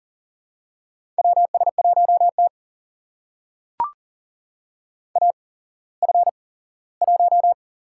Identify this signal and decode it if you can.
Morse code, heard as a keyed tone, about 38 wpm, spaced as for 19 wpm, 700 Hz. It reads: WS1T AF1
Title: USA Call Signs - 4 Characters - 8x Character Spacing - 40wpm